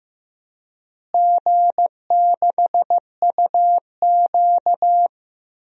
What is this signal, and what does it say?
Morse code, heard as a keyed tone, about 15 wpm, 700 Hz. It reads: G6UQ
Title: DXCC-100 Call Signs - 4 Characters - Repeated 3x - 15wpm